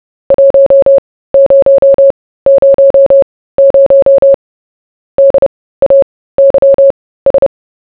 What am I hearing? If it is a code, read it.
Morse code, heard as a keyed tone, about 30 wpm, spaced as for 17 wpm, 550 Hz. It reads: 1000 DAYS